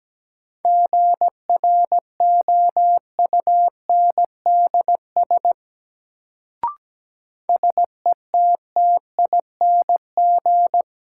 Morse code, heard as a keyed tone, about 17 wpm, 700 Hz. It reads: GROUNDS SETTING
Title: Sentences from Top 1300 Words - New Word Drill Alt - 17wpm